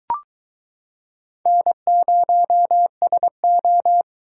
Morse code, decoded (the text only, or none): N0SO